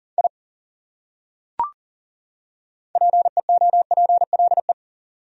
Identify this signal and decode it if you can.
Morse code, heard as a keyed tone, about 37 wpm, 700 Hz. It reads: I PEOPLE
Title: Top 100 Words 40wpm